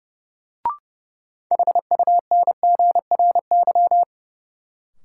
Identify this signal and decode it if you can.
Morse code, heard as a keyed tone, about 30 wpm, 700 Hz. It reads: HUNGRY
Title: Sentences from Top 1400 Words - New Word Drill Alt - 30wpm